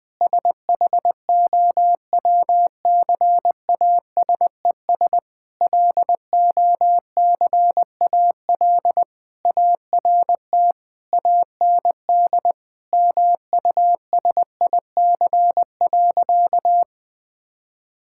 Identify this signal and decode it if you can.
Morse code, heard as a keyed tone, about 20 wpm, 700 Hz. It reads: SHOWCASES LOCAL ART AND MUSIC.
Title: Difficult Sentences with Hints - 20wpm